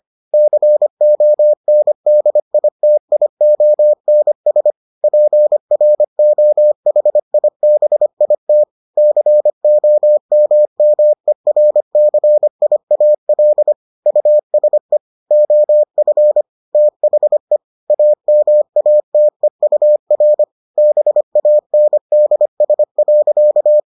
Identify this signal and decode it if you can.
Morse code, heard as a keyed tone, about 25 wpm, 600 Hz. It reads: CONDITIONS PROHIBIT COMMERCIAL USE OF THE AMATEUR BANDS.